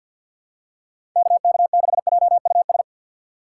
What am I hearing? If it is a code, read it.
Morse code, heard as a keyed tone, about 47 wpm, 700 Hz. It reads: KK6JUS
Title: USA Call Signs - 6x Character Spacing - 50wpm